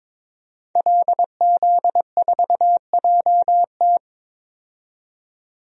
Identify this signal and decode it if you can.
Morse code, heard as a keyed tone, about 21 wpm, 700 Hz. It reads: LZ4JT
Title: DXCC-100 Call Signs - 5 Characters - 22wpm